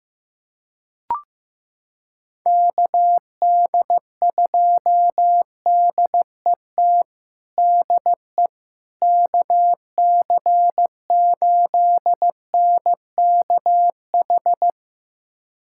Morse code, heard as a keyed tone, about 15 wpm, 700 Hz. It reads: KD2DET DE KC8NKH
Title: USA Callsign DE Callsign - 6 Characters - 15wpm